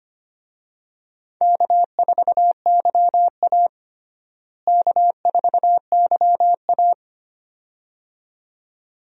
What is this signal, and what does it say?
Morse code, heard as a keyed tone, about 25 wpm, 700 Hz. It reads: K4YA K4YA
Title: USA Call Signs - 4 Characters - Repeated 2x - 25wpm